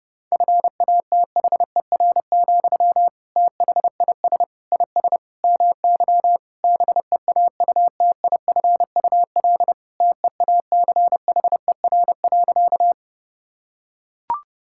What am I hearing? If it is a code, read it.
Morse code, heard as a keyed tone, about 32 wpm, 700 Hz. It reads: FATHER, THIS IS MY BEAUTIFUL TEACHER.
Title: Sentences from Top 1000 Words - Review Part 6 - 30wpm